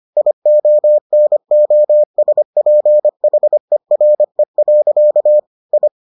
Morse code, heard as a keyed tone, about 25 wpm, 600 Hz. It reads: IONOSPHERE. I